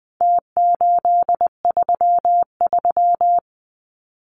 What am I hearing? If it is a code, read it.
Morse code, heard as a keyed tone, about 20 wpm, 700 Hz. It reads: T833
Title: Numbers - Contest Style - 4-Digit 20wpm